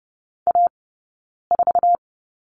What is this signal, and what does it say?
Morse code, heard as a keyed tone, about 30 wpm, spaced as for 9 wpm, 690 Hz. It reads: A4
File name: Lesson 18d - Letter m - Two Characters 30wpm